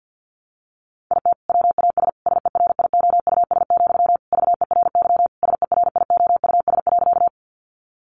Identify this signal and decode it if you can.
Morse code, heard as a keyed tone, about 50 wpm, 700 Hz. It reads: IT WAS SERIOUS, VERY SERIOUS.
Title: Sentences from Top 1100 Words 50wpm